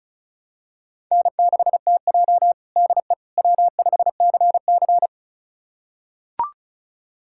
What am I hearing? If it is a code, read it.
Morse code, heard as a keyed tone, about 34 wpm, 700 Hz. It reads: N6TJ DE W5CC
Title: USA Callsign DE Callsign - 4 Characters - 35wpm